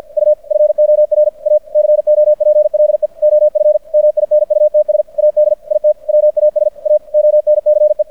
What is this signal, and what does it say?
Morse code, heard as a keyed tone, about 50 wpm, 600 Hz. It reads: U FOR A COPLE OF MINUTS AD IT WAS A ONCE
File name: archived / CRSnet